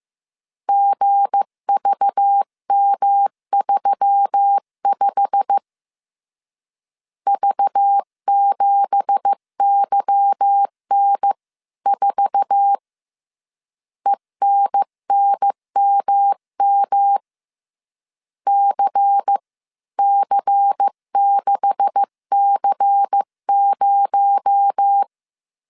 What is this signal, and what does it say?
Morse code, heard as a keyed tone, about 15 wpm, 800 Hz. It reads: GVM35 V7YN4 ENNMM CC6C0
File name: CW / DET / 1 9 1500C S B 14wpm 3302